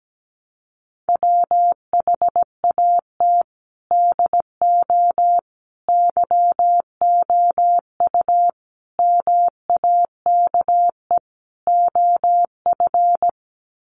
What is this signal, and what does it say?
Morse code, heard as a keyed tone, about 17 wpm, 700 Hz. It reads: WHAT DO YOU MAKE OF
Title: Sets of 5 Words 17wpm